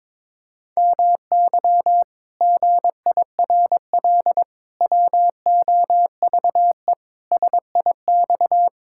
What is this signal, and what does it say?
Morse code, heard as a keyed tone, about 21 wpm, 700 Hz. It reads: MY GIRL WOVE SIX
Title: Pangram Challenge v2 - 22wpm